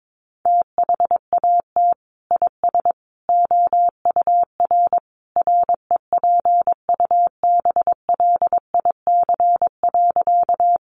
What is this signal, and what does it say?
Morse code, heard as a keyed tone, about 22 wpm, 700 Hz. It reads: THAT IS OUR REPUBLIC.
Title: Sentences from Top 1500 Words 22wpm